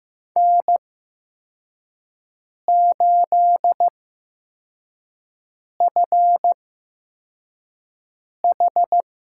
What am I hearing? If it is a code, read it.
Morse code, heard as a keyed tone, about 15 wpm, spaced as for 4 wpm, 700 Hz. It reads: N8FH
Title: USA Call Signs - 5 Characters - 8x Character Spacing - 15wpm